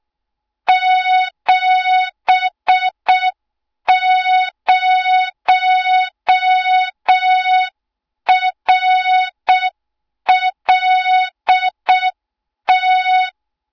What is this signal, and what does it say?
Morse code, heard as a keyed tone, about 6 wpm, 3700 Hz. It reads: 70RLT
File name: Morze-HA2MN / hang / 20 GYAKORLAT